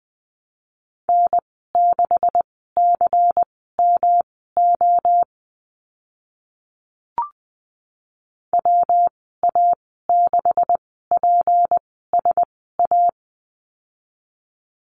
Morse code, heard as a keyed tone, about 20 wpm, spaced as for 14 wpm, 700 Hz. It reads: N6CMO WA6PSA
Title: 167 Lesson 27g - Number 7 - Call Signs 20wpm